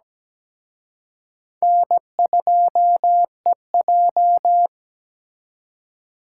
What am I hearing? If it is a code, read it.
Morse code, heard as a keyed tone, about 20 wpm, 700 Hz. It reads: N2EJ
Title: Speed Racing - USA 1x2 Call Signs 17wpm